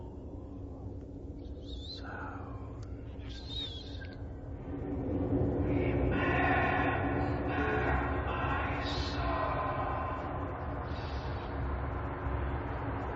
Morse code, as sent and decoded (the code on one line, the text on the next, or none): none